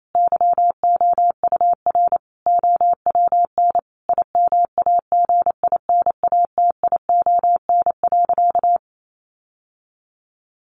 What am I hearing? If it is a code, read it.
Morse code, heard as a keyed tone, about 28 wpm, 700 Hz. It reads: YOUR OWN IMAGINATION.
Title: Sentences from Top 1300 Words 28wpm